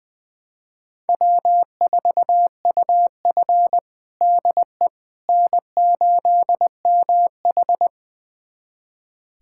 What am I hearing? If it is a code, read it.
Morse code, heard as a keyed tone, about 20 wpm, 700 Hz. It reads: W4UF DE N8MH
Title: USA Callsign DE Callsign - 4 Characters - 20wpm